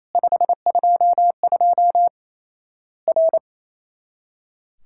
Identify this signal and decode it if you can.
Morse code, heard as a keyed tone, about 28 wpm, 700 Hz. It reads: 522 R
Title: Worked All Europe DX Contest - QTC Exchange - 28wpm